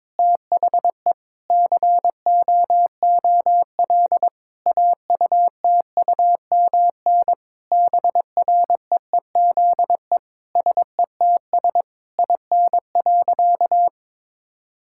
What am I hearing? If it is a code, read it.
Morse code, heard as a keyed tone, about 22 wpm, 700 Hz. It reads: THE COOL AUTUMN BREEZE SETS IN.
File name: Difficult Sentences with Hints - 22wpm